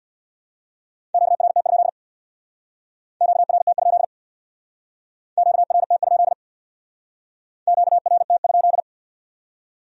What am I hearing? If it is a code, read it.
Morse code, heard as a keyed tone, about 62 wpm, 700 Hz. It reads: QRT? QRT? QRT? QRT?
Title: Speed-Racing Q-Codes - 50wpm